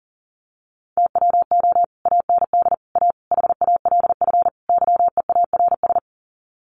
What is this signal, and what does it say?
Morse code, heard as a keyed tone, about 40 wpm, 700 Hz. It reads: TWO AND A HALF YEARS
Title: Sets of 5 Words 40wpm